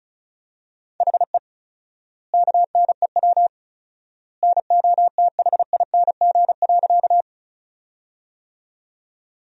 Morse code, heard as a keyed tone, about 34 wpm, 700 Hz. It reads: HE KNEW NOTHING.
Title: 4x Word-Spacing - Sentences from Top 500 Words - 35wpm